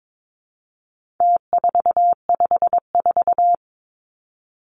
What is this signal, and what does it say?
Morse code, heard as a keyed tone, about 22 wpm, 700 Hz. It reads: T454
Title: Numbers - Contest Style - 4-Digit 22wpm